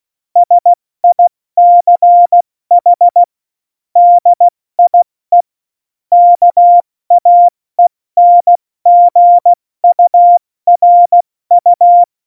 Morse code, heard as a keyed tone, about 16 wpm, spaced as for 14 wpm, 700 Hz. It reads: SICH DIE KAENGURU
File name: Morsekurs / Witze / Witz 16. Oktober 2018, 70 F80